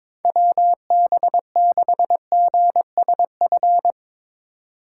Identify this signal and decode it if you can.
Morse code, heard as a keyed tone, about 24 wpm, 700 Hz. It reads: WB6GSF
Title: Speed Racing - USA Call Signs 22wpm